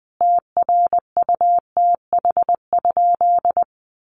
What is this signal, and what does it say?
Morse code, heard as a keyed tone, about 20 wpm, 700 Hz. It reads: TRUTH?
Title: Sentences from Top 800 Words 20wpm